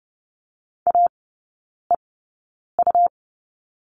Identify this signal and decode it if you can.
Morse code, heard as a keyed tone, about 30 wpm, spaced as for 9 wpm, 700 Hz. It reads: AEU
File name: Lesson 18e - Letter m - Three Characters 30wpm